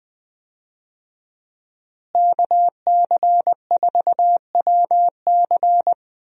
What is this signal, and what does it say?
Morse code, heard as a keyed tone, about 20 wpm, 700 Hz. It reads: KC4WC
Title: USA Call Signs - 5 Characters - Repeated 3x - 20wpm